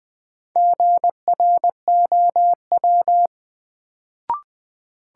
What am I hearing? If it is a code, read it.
Morse code, heard as a keyed tone, about 20 wpm, 700 Hz. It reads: GROW
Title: Sentences from Top 300 Words - New Word Drill - 20wpm